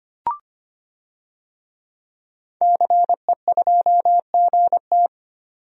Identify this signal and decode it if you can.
Morse code, heard as a keyed tone, about 25 wpm, 700 Hz. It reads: CE2GT
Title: DXCC-100 Call Signs - 5 Characters - 25wpm